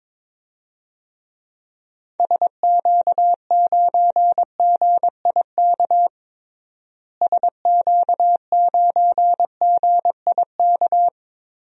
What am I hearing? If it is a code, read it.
Morse code, heard as a keyed tone, about 22 wpm, 700 Hz. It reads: SQ9GIK SQ9GIK
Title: DXCC-100 Call Signs - 6 Characters - Repeated 3x - 22wpm